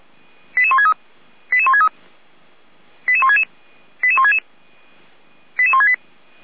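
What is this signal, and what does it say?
Five-tone selective call (ZVEI), heard as quick runs of five tones, tones of about 70 ms. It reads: 88163 88163 88166 88166 88168
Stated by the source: ZVEI five-tone selective call received with an SDR.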